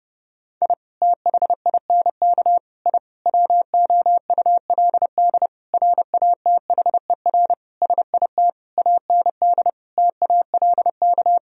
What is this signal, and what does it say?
Morse code, heard as a keyed tone, about 30 wpm, 700 Hz. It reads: I THINK I WOULD RATHER SIT AND TALK
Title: Sentences from Top 1000 Words - Review Part 4 - 30wpm